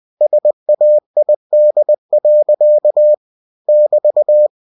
Morse code, heard as a keyed tone, about 20 wpm, 600 Hz. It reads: SAID. <BT>